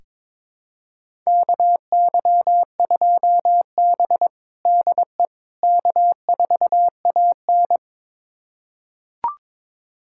Morse code, heard as a keyed tone, about 21 wpm, 700 Hz. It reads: KY2B DE K4AN
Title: USA Callsign DE Callsign - 4 Characters - 22wpm